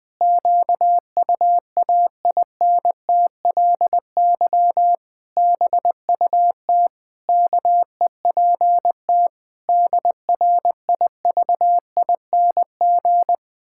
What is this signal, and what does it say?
Morse code, heard as a keyed tone, about 21 wpm, 700 Hz. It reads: QUAINTLY BUT KEPT DRIVING
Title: Pangram Challenge v2 - 20wpm